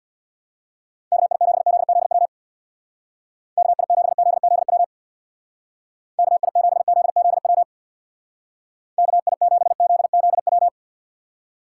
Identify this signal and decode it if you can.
Morse code, heard as a keyed tone, about 63 wpm, 700 Hz. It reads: KI7ZZW KI7ZZW KI7ZZW KI7ZZW